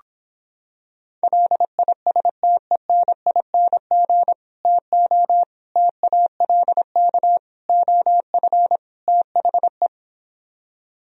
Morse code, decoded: LISTENING TO TALK OF THE

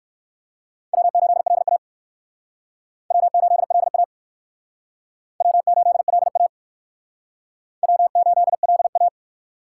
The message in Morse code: W8LA W8LA W8LA W8LA

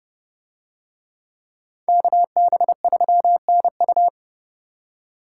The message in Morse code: KB3NU

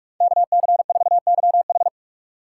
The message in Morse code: KC4YH

